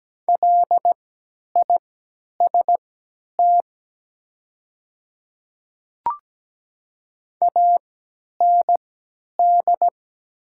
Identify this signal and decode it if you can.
Morse code, heard as a keyed tone, about 17 wpm, spaced as for 10 wpm, 700 Hz. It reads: LIST AND
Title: 3x Character Spacing - Top 500 Words - 17wpm